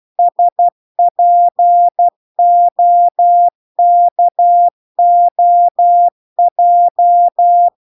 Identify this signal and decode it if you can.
Morse code, heard as a keyed tone, about 12 wpm, 700 Hz. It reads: SPOKOJ